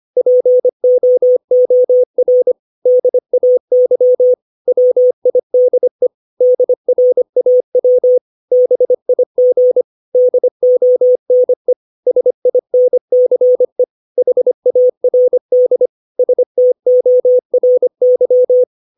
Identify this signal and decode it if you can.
Morse code, heard as a keyed tone, about 25 wpm, 500 Hz. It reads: POOR DAY WIDE DRAW BIG DONE SINCE HARD STORY
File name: 632 words / 632 Words 25WPM